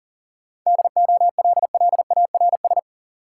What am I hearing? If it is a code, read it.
Morse code, heard as a keyed tone, about 40 wpm, 700 Hz. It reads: DOLLARS
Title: Sets of 5 Words - New Word Drill - 40wpm